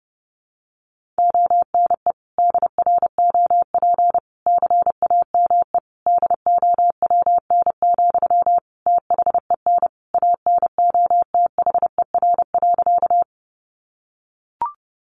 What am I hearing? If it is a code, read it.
Morse code, heard as a keyed tone, about 30 wpm, 700 Hz. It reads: ONE DROP CAME DOWN, THEN ANOTHER.
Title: Sentences from Top 1400 Words 30wpm